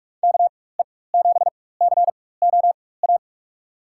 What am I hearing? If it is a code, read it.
Morse code, heard as a keyed tone, about 43 wpm, spaced as for 21 wpm, 700 Hz. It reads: KE7COA
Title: USA Call Signs - 6 Characters - 4x Character Spacing - 45wpm